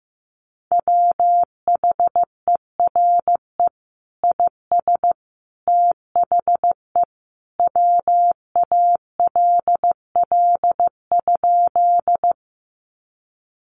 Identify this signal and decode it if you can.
Morse code, heard as a keyed tone, about 15 wpm, 700 Hz. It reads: WHERE IS THE WALL?
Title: Sentences from Top 1100 Words 15wpm